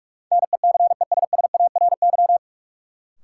Morse code, heard as a keyed tone, about 44 wpm, 700 Hz. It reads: NECESSARY
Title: Sentences from Top 800 Words - New Word Drill - 45wpm